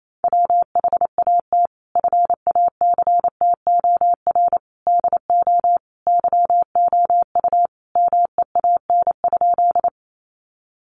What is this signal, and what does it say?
Morse code, heard as a keyed tone, about 28 wpm, 700 Hz. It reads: WHAT FACTOR DO YOU MEAN?
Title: Sentences from Top 1300 Words 28wpm